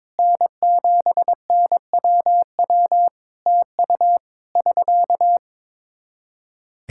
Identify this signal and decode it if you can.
Morse code, heard as a keyed tone, about 22 wpm, 700 Hz. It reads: N7NWW TU <SK>
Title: Practice QSO - Exchange 4 - Ending - 22wpm